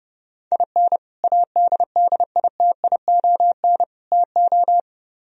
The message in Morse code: IN ADDITION TO

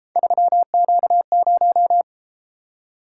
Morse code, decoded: 3Q0

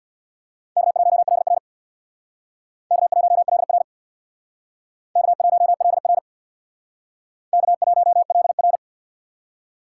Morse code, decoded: K1LR K1LR K1LR K1LR

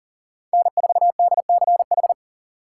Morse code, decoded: N4DCH

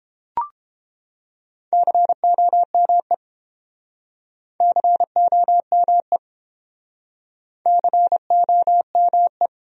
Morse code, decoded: COME COME COME

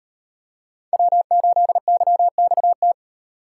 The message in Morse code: W8YXT

